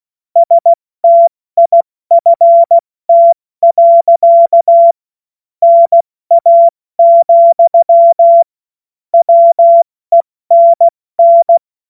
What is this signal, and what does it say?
Morse code, heard as a keyed tone, about 16 wpm, 680 Hz. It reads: STIFT. NA, WENN
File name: Morsekurs / Witze / Witz 20. Januar 2026 70 F80